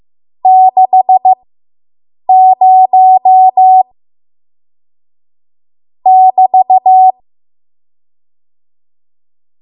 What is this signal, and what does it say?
Morse code, heard as a keyed tone, about 15 wpm, spaced as for 7 wpm, 750 Hz. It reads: 60 <BT>